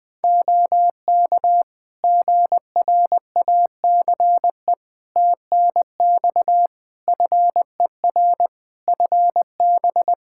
Morse code, heard as a keyed tone, about 21 wpm, 700 Hz. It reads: OK GRACE TNX FER FB
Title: Practice QSO - Exchange 4 - Ending - 20wpm